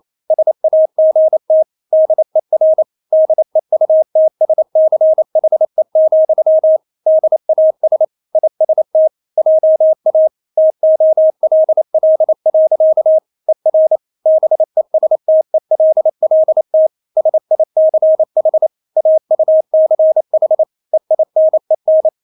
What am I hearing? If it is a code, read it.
Morse code, heard as a keyed tone, about 28 wpm, 630 Hz. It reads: SAGT DER DEUTSCHE, DAS IST JA TOLL. ER BESTELLT SICH AUCH EINEN
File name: Witz9 28wpm